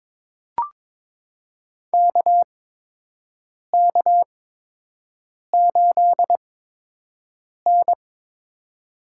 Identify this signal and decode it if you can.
Morse code, heard as a keyed tone, about 22 wpm, spaced as for 6 wpm, 700 Hz. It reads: KK8N
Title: USA Call Signs - 4 Characters - 8x Character Spacing - 22wpm